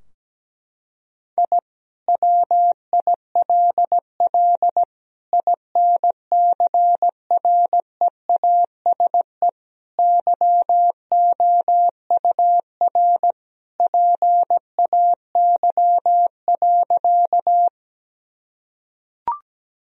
Morse code, heard as a keyed tone, about 17 wpm, 700 Hz. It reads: I WILL INCREASE YOUR PAY.
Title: Sentences from Top 1000 Words - Review Part 4 - 17wpm